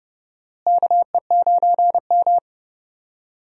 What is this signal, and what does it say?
Morse code, heard as a keyed tone, about 32 wpm, 700 Hz. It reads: KE9M
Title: USA Call Signs - 4 Characters - Repeated 3x - 30wpm